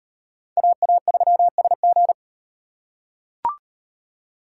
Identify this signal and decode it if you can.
Morse code, heard as a keyed tone, about 37 wpm, 700 Hz. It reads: AA3SG